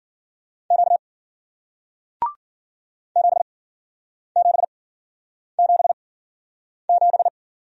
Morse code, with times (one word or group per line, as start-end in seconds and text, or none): 0.7-1.0 Q
3.2-7.3 7777